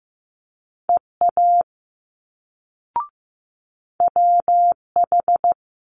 Morse code, heard as a keyed tone, about 15 wpm, 700 Hz. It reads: EA WH
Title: Multi-letter Phonemes Quick Fire 15wpm